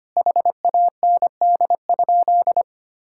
Morse code, heard as a keyed tone, about 25 wpm, 700 Hz. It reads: HAND?
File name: practice / 25wpm / Sentences with 9-Word Phrases - 25wpm